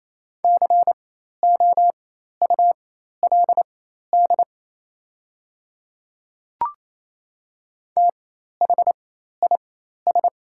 Morse code, heard as a keyed tone, about 27 wpm, spaced as for 13 wpm, 690 Hz. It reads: COULD THIS
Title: 4x Character Spacing - Top 100 Words - 28wpm